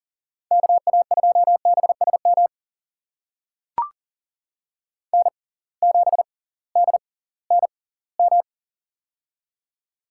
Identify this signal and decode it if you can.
Morse code, heard as a keyed tone, about 40 wpm, spaced as for 22 wpm, 700 Hz. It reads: KA2BSM N7DNM
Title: USA Call Signs - 6x Character Spacing - 40wpm